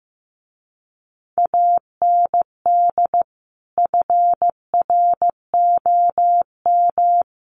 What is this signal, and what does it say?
Morse code, heard as a keyed tone, about 14 wpm, 700 Hz. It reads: AND FROM
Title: A Sound Of Thunder 15wpm